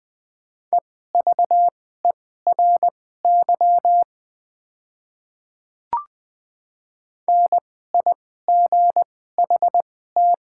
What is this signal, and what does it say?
Morse code, heard as a keyed tone, about 20 wpm, spaced as for 14 wpm, 700 Hz. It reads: EVERY NIGHT
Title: 2x Character Spacing - Top 300 Words - 20wpm